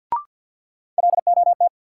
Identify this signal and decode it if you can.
Morse code, heard as a keyed tone, about 51 wpm, 700 Hz. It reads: LOT
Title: Sets of 2 Words - New Word Drill - 50wpm